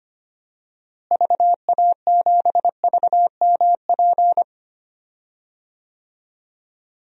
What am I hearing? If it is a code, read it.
Morse code, heard as a keyed tone, about 25 wpm, 700 Hz. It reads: VA7VMP
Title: DXCC-100 Call Signs - 6 Characters - Repeated 2x - 25wpm